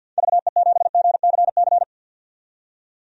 Morse code, heard as a keyed tone, about 51 wpm, 700 Hz. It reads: VE7GCC
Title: DXCC-100 Call Signs - 6 Characters - 50wpm